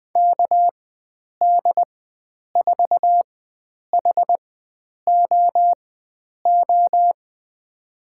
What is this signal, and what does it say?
Morse code, heard as a keyed tone, about 20 wpm, spaced as for 9 wpm, 700 Hz. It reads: KD4HOO
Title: USA Call Signs - 6 Characters - 4x Character Spacing - 20wpm